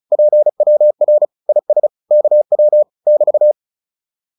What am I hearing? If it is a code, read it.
Morse code, heard as a keyed tone, about 34 wpm, 600 Hz. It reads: PWR IS KW <BT>